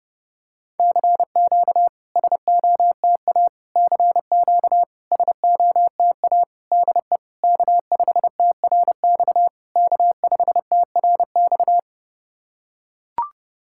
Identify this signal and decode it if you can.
Morse code, heard as a keyed tone, about 30 wpm, 700 Hz. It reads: CQ SOTA CQ SOTA DE K5TRX K5TRX
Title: SOTA QSO Practice 30wpm